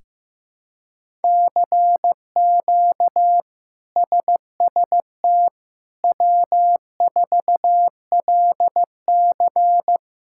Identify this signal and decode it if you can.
Morse code, heard as a keyed tone, about 15 wpm, 700 Hz. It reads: CQ SST W4LC
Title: K1USN SST QSO Practice - 15wpm